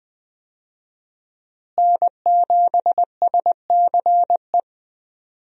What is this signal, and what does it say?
Morse code, heard as a keyed tone, about 20 wpm, 700 Hz. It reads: N7SCE
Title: USA Call Signs - 5 Characters - Repeated 3x - 20wpm